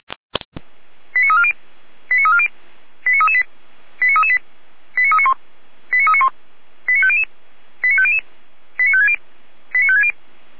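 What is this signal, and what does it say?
Five-tone selective call (ZVEI), heard as quick runs of five tones, tones of about 70 ms. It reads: 89340 89340 89338 89338 89391 89391 89500 89500 89560 89560
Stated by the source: ZVEI five-tone selective call received on an SDR.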